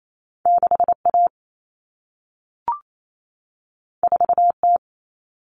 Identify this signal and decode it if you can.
Morse code, heard as a keyed tone, about 28 wpm, 700 Hz. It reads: 6A 4T
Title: Common Cut Numbers - 2-Digit 28wpm